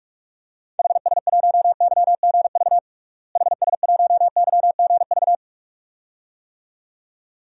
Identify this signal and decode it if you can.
Morse code, heard as a keyed tone, about 47 wpm, 700 Hz. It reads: HS1YGV HS1YGV
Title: DXCC-100 Call Signs - 6 Characters - Repeated 2x - 45wpm